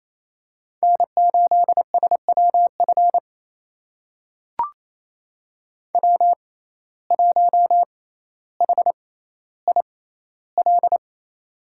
Morse code, heard as a keyed tone, about 28 wpm, spaced as for 14 wpm, 700 Hz. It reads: N8SWF W1HIL
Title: USA Call Signs - 5 Characters - 6x Character Spacing - 28wpm